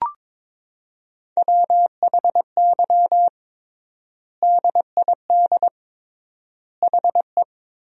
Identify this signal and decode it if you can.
Morse code, heard as a keyed tone, about 22 wpm, 700 Hz. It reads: WHY DID HE
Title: 3x Word-Spacing - Sentences from Top 200 Words - 22wpm